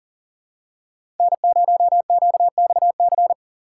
Morse code, heard as a keyed tone, about 40 wpm, 700 Hz. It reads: N0QXC
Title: USA Call Signs - 6x Character Spacing - 40wpm